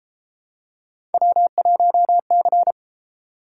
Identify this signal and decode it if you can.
Morse code, heard as a keyed tone, about 33 wpm, 700 Hz. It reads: W1C